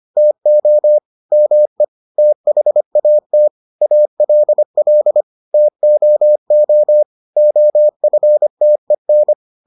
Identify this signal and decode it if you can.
Morse code, heard as a keyed tone, about 25 wpm, 600 Hz. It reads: TO ME THAT ALL TOO OFTEN